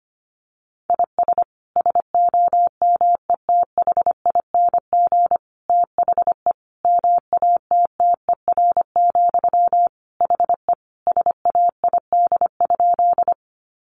Morse code, heard as a keyed tone, about 25 wpm, 700 Hz. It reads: IS SOMETHING THE MATTER, HE SAID?
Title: Sentences from Top 700 Words 25wpm